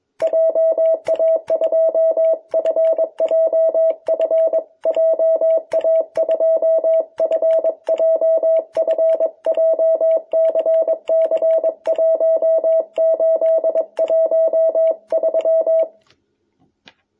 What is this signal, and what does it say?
Morse code, heard as a keyed tone, about 21 wpm, 660 Hz. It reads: JA2FJF JA2FJF JCC1813
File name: jh8cbh / 2021cw